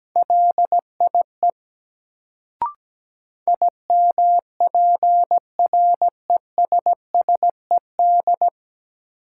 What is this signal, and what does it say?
Morse code, heard as a keyed tone, about 17 wpm, 700 Hz. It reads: LIE IMPRESSED
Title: Sentences from Top 1600 Words - New Word Drill Alt - 17wpm